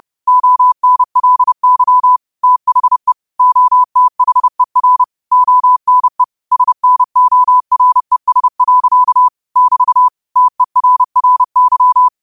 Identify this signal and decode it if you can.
Morse code, heard as a keyed tone, about 30 wpm, 1000 Hz. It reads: ONLY THE OTHER ONE SNORES. <BT> TERRY
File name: qotdfiles / 30wpm